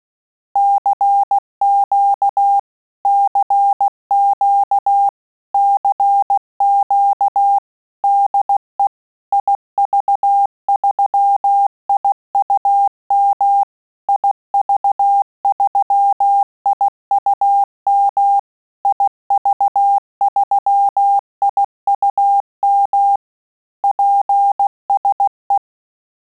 Morse code, heard as a keyed tone, about 16 wpm, 800 Hz. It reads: CQ CQ CQ DE IV3IUM IV3IUM IV3IUM PSE
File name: Telegraphy-group / QSO CW 15 PPM